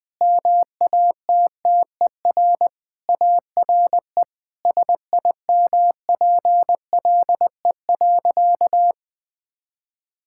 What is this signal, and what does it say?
Morse code, heard as a keyed tone, about 21 wpm, 700 Hz. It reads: MATTER ARE SIMPLE.
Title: Sentences with 3-Word Phrases - 20wpm